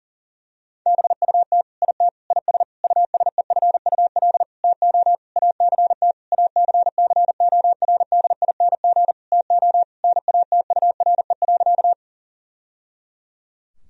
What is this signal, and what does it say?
Morse code, heard as a keyed tone, about 40 wpm, 700 Hz. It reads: BUT IT IS USEFUL TO ACT ACCORDING TO NATURE.
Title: Sentences from Top 1000 Words - Review Part 4 - 40wpm